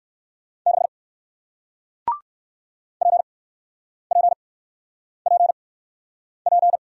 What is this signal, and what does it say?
Morse code, heard as a keyed tone, about 61 wpm, spaced as for 9 wpm, 700 Hz. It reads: 6 PPPP